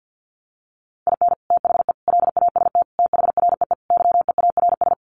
Morse code, heard as a keyed tone, about 50 wpm, 700 Hz. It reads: IN THE LAST THREE YEARS
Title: Sets of 5 Words Encore2 50wpm